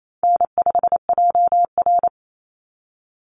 Morse code, heard as a keyed tone, about 28 wpm, 700 Hz. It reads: N5JR
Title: Call Signs Repeated 3x - 28wpm